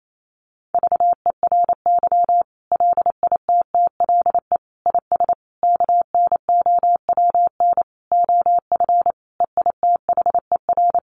Morse code, heard as a keyed tone, about 28 wpm, 700 Hz. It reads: VERY LITTLE IS KNOWN OF EITHER
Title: Sentences from Top 1500 Words 28wpm